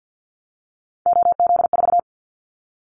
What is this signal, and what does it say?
Morse code, heard as a keyed tone, about 50 wpm, 700 Hz. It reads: OZ4
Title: Callsign Prefixes for Top 100 DXCC - 50wpm